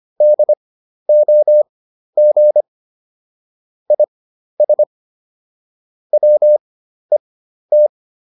Morse code, heard as a keyed tone, about 25 wpm, spaced as for 12 wpm, 600 Hz. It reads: DOG IS WET